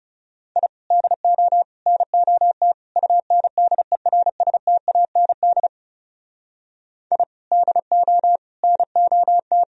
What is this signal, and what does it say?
Morse code, heard as a keyed tone, about 34 wpm, 700 Hz. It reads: I DO NOT UNDERSTAND I DO NOT
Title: Speed-Racing - Sets of 4 Words 30wpm